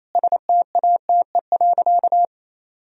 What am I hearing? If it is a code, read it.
Morse code, heard as a keyed tone, about 28 wpm, 700 Hz. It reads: STATE.
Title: Sentences from Top 1000 Words - Review Part 2 - 28wpm